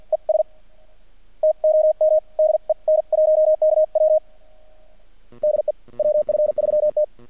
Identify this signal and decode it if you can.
Morse code, heard as a keyed tone, about 50 wpm, spaced as for 44 wpm, 620 Hz. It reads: ES TOM DE N1KW DE WF3T
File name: archived / CRSnet